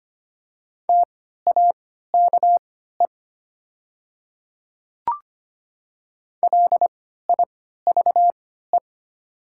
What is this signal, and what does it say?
Morse code, heard as a keyed tone, about 25 wpm, spaced as for 14 wpm, 690 Hz. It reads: TAKE LIVE
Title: 3x Character Spacing - Top 300 Words - 25wpm